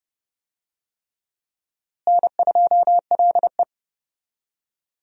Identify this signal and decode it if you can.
Morse code, heard as a keyed tone, about 31 wpm, 700 Hz. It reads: N2LE
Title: USA Call Signs - 4 Characters - Repeated 3x - 30wpm